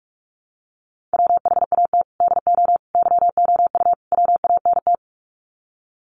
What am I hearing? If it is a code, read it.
Morse code, heard as a keyed tone, about 44 wpm, 700 Hz. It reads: WHAT DO YOU WANT
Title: Sets of 4 Words 45wpm